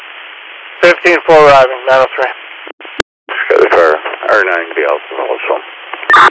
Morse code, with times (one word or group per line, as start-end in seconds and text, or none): none